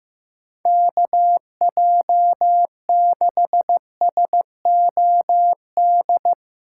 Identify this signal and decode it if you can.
Morse code, heard as a keyed tone, about 15 wpm, 700 Hz. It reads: KJ6SOD
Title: USA Call Signs - 6 Characters - 8x Character Spacing - 15wpm